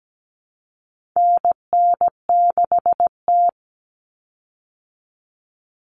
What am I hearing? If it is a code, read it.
Morse code, heard as a keyed tone, about 17 wpm, 700 Hz. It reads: NN6T
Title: Call Signs - 17wpm